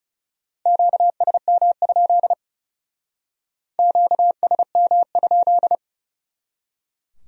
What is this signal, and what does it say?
Morse code, heard as a keyed tone, about 37 wpm, 700 Hz. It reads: QSM? QSM?
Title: Speed-Racing Q-Codes - 30wpm